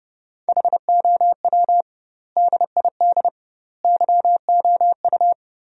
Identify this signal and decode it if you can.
Morse code, heard as a keyed tone, about 30 wpm, 700 Hz. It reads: HOW DID YOU
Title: 2x Word-Spacing - Sentences from Top 100 Words - 30wpm